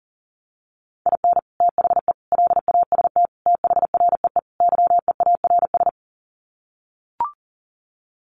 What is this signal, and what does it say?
Morse code, heard as a keyed tone, about 40 wpm, 700 Hz. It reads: IN THE LAST THREE YEARS
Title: Sets of 5 Words Encore2 40wpm